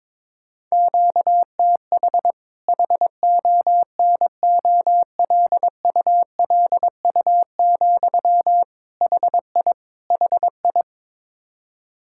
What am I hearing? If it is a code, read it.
Morse code, heard as a keyed tone, about 22 wpm, 700 Hz. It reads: QTH HONOLULU, HI HI
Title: Practice QSO - Exchange 1 - Beginning - 22wpm